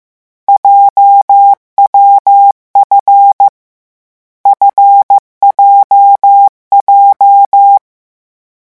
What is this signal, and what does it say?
Morse code, heard as a keyed tone, about 15 wpm, 800 Hz. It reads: JWF FJJ
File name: morse mp3 / lecon7-vitesse 14